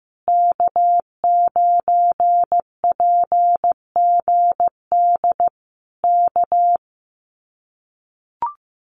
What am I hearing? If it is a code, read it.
Morse code, heard as a keyed tone, about 15 wpm, 700 Hz. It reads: K9PGD K
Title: Common QSO Phrases Encore 15wpm